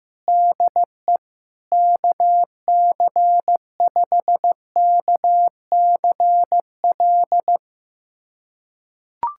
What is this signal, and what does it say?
Morse code, heard as a keyed tone, about 15 wpm, 700 Hz. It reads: DE KC5KCL
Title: USA Callsign DE Callsign 15wpm